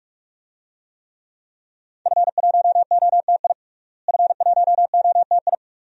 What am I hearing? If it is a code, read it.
Morse code, heard as a keyed tone, about 46 wpm, 700 Hz. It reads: F1OTI F1OTI
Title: DXCC-100 Call Signs - 5 Characters - Repeated 2x - 45wpm